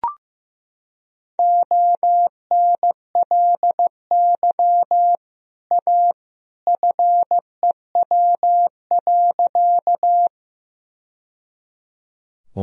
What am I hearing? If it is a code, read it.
Morse code, heard as a keyed tone, about 15 wpm, 700 Hz. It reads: ONLY A FEW.
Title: Sentences from Top 1000 Words - Review Part 1 - 15wpm